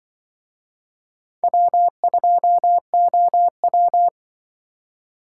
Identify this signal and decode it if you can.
Morse code, heard as a keyed tone, about 24 wpm, 700 Hz. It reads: W2OW
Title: Speed Racing - USA 1x2 Call Signs 20wpm